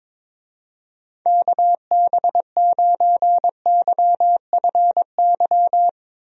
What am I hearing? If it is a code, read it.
Morse code, heard as a keyed tone, about 22 wpm, 700 Hz. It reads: KB9YFY
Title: USA Call Signs - 6 Characters - Repeated 3x - 22wpm